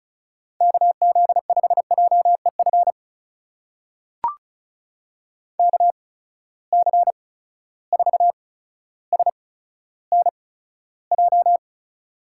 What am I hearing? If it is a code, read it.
Morse code, heard as a keyed tone, about 36 wpm, spaced as for 15 wpm, 700 Hz. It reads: KZ5JEF KC4SNJ
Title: USA Call Signs - 6 Characters - 8x Character Spacing - 35wpm